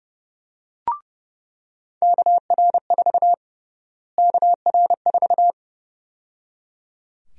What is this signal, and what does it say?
Morse code, heard as a keyed tone, about 30 wpm, 700 Hz. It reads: KR4 KR4
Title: USA Call Sign Prefixes - Repeated 2x - 30wpm